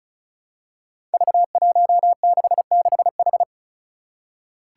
Morse code, heard as a keyed tone, about 36 wpm, 700 Hz. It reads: V166H
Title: Five Challenging Letter-Number Combinations 35wpm